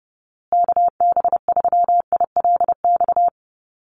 Morse code, heard as a keyed tone, about 30 wpm, 700 Hz. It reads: KB3ILX
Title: USA Call Signs Repeated 3x 30wpm